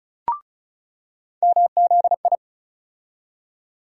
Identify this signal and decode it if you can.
Morse code, heard as a keyed tone, about 34 wpm, 700 Hz. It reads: MZI
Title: USA Call Sign Postfixes - 3 Letters 35wpm